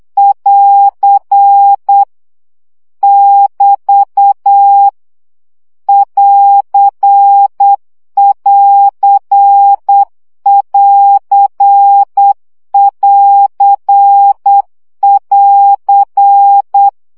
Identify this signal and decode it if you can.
Morse code, heard as a keyed tone, about 8 wpm, 800 Hz. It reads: <AR> <BT> <AR><AR><AR><AR><AR>
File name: CW gyakorlas plusz 7 baud